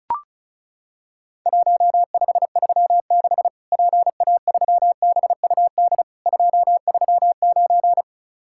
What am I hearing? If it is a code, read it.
Morse code, heard as a keyed tone, about 36 wpm, 700 Hz. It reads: 1536 PA3BUD 239
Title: Worked All Europe DX Contest - QTC Exchange - 35wpm